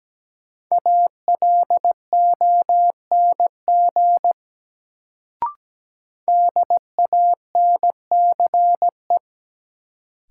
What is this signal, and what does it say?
Morse code, heard as a keyed tone, about 17 wpm, 700 Hz. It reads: ALONG DANCE
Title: Sentences from Top 700 Words - New Word Drill - 17wpm